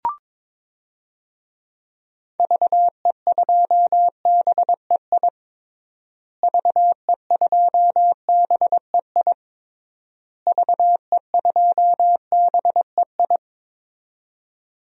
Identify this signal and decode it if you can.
Morse code, heard as a keyed tone, about 22 wpm, 700 Hz. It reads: VE2BEI VE2BEI VE2BEI
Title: DXCC-100 Call Signs - 6 Characters - Repeated 3x - 22wpm